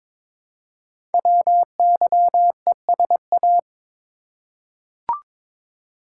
Morse code, heard as a keyed tone, about 22 wpm, 700 Hz. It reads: WYESA
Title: Code Groups - 5 Letters - 22wpm